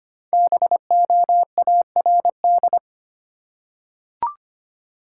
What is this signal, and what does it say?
Morse code, heard as a keyed tone, about 25 wpm, 700 Hz. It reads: BOARD
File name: 5 Letter Words - First 3 Letters Given - 25wpm